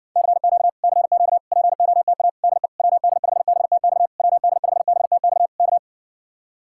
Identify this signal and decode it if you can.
Morse code, heard as a keyed tone, about 59 wpm, 700 Hz. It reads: CQ CQ POTA DE WD5BTX WD5BTX K